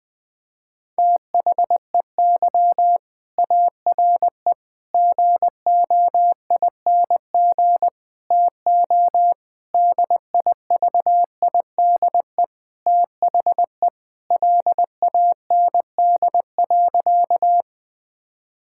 Morse code, heard as a keyed tone, about 20 wpm, 700 Hz. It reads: THEY ARE GOING TO DIVIDE THE LAND.
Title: Sentences from Top 1600 Words 20wpm